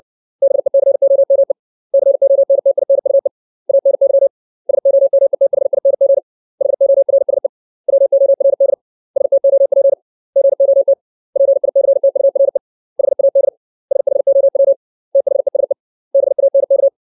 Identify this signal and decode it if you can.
Morse code, heard as a keyed tone, about 60 wpm, 550 Hz. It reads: BECOME CONTINUE ANY SOMETHING HORSE WORD STOP GOT PICTURE HAD SHOW THESE BANK
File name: 632 words / 632 Words 60WPM